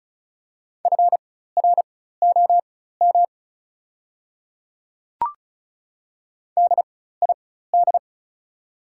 Morse code, heard as a keyed tone, about 34 wpm, spaced as for 16 wpm, 690 Hz. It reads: FROM DID
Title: 4x Character Spacing - Top 100 Words - 35wpm